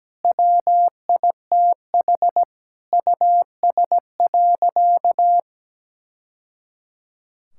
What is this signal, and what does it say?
Morse code, heard as a keyed tone, about 17 wpm, 700 Hz. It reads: WITH US.
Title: Sentences from Top 1000 Words - Review Part 6 - 17wpm